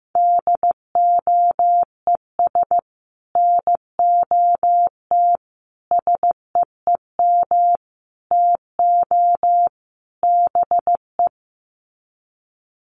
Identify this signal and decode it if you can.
Morse code, heard as a keyed tone, about 15 wpm, 700 Hz. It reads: DOES NOT SEEM TO BE
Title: Sets of 5 Words 15wpm